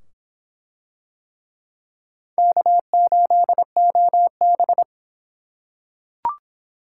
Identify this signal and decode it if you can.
Morse code, heard as a keyed tone, about 24 wpm, 700 Hz. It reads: K8OB